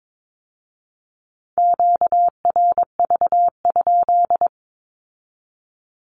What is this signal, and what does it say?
Morse code, heard as a keyed tone, about 22 wpm, 700 Hz. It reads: QRV?
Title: Q-Codes with Spoken Meaning - 22wpm